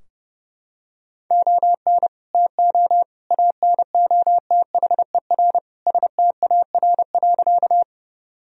Morse code, heard as a keyed tone, about 30 wpm, 700 Hz. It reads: ON TO ANOTHER STAR.
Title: Sentences from Top 1000 Words - Review Part 2 - 30wpm